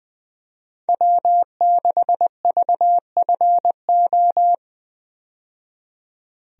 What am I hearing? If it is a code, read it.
Morse code, heard as a keyed tone, about 20 wpm, 700 Hz. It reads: W6VFO
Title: USA Call Signs - 5 Characters - 20wpm